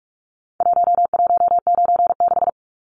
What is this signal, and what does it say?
Morse code, heard as a keyed tone, about 44 wpm, 700 Hz. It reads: 1196
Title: Numbers - Contest Style - 4-Digit 45wpm